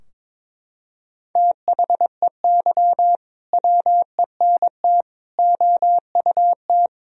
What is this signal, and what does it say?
Morse code, heard as a keyed tone, about 22 wpm, 700 Hz. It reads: THEY WENT OUT
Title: Sentences from Top 1000 Words - Review Part 3 - 22wpm